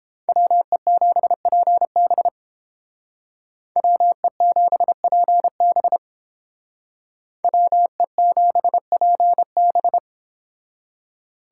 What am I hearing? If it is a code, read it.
Morse code, heard as a keyed tone, about 31 wpm, 700 Hz. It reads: WE7PB WE7PB WE7PB